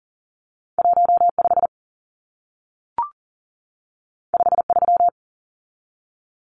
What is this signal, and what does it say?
Morse code, heard as a keyed tone, about 40 wpm, 700 Hz. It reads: 15 53
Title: Numbers - 2-Digit 40wpm